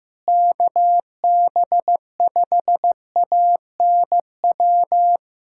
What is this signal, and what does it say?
Morse code, heard as a keyed tone, about 15 wpm, 700 Hz. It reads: KB5ANW
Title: USA Callsign DE Callsign - 6 Characters - 15wpm